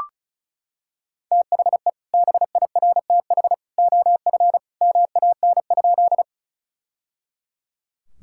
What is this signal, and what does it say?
Morse code, heard as a keyed tone, about 36 wpm, 700 Hz. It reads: THE BIRTH OF MAN?
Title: Sentences from Top 1000 Words - Review Part 5 - 35wpm